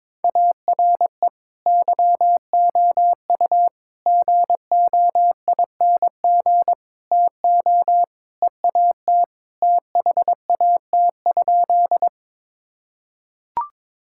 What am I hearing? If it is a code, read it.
Morse code, heard as a keyed tone, about 22 wpm, 700 Hz. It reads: ARE YOU GOING TO EAT THAT?
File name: practice / 22wpm / Sentences with 4-Word Phrases - 22wpm